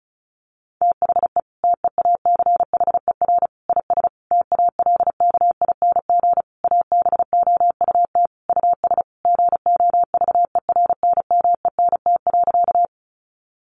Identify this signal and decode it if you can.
Morse code, heard as a keyed tone, about 34 wpm, 700 Hz. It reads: THE TEACHER IS TALKING ABOUT US GOVERNMENT.
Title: Easy Phrases 35wpm